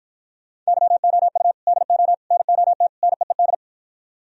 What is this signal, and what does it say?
Morse code, heard as a keyed tone, about 54 wpm, 700 Hz. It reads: YOU DO NOT NEED